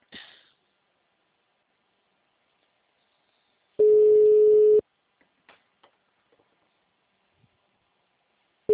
Ringback tone (Europe, 425 Hz).
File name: out- -2036- - - .13280